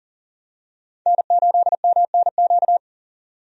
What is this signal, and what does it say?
Morse code, heard as a keyed tone, about 40 wpm, 700 Hz. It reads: N8MNQ
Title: USA Call Signs - 5 Characters - 40wpm